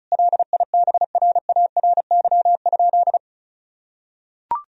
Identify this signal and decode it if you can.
Morse code, heard as a keyed tone, about 34 wpm, 700 Hz. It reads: LIBRARY?
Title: Sentences from Top 1000 Words - Review Part 4 - 35wpm